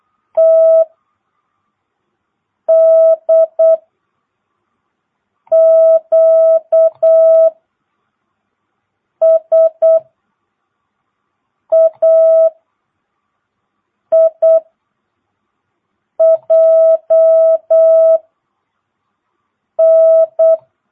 Morse code, heard as a keyed tone, about 8 wpm, 660 Hz. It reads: T D Q S A I J N
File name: jh8cbh / 2022cw